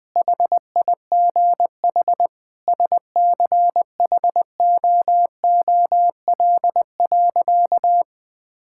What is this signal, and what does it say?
Morse code, heard as a keyed tone, about 20 wpm, 700 Hz. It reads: HIGH SCHOOL.
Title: Sentences with 2-Word Phrases - 20wpm